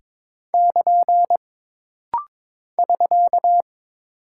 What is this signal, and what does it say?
Morse code, heard as a keyed tone, about 22 wpm, 700 Hz. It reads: <KN> <SK>
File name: practice / 22wpm / Single Punctuation-Prosign - Rapid-Fire - 22wpm